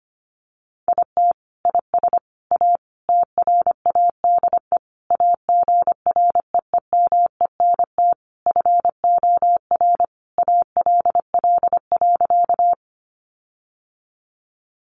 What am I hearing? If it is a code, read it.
Morse code, heard as a keyed tone, about 25 wpm, 700 Hz. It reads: IT IS A TRADE AGREEMENT FOR ALL.
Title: Sentences from Top 1500 Words 25wpm